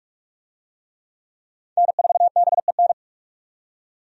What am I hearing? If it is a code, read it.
Morse code, heard as a keyed tone, about 41 wpm, 700 Hz. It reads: N4BEN
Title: USA Call Signs - 5 Characters - 45wpm